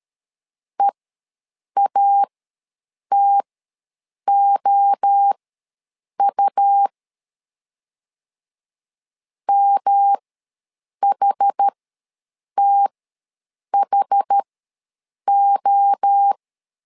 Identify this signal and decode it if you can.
Morse code, heard as a keyed tone, about 13 wpm, spaced as for 7 wpm, 800 Hz. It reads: EATOU MHTHO
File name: LEZIONI / 3 MWRO / 3 MWRO ws 12wpm 1914